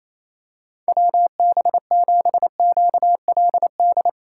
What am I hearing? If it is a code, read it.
Morse code, heard as a keyed tone, about 28 wpm, 700 Hz. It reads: WB7QLD
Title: USA Call Signs - 6 Characters - 2x Character Spacing - 28wpm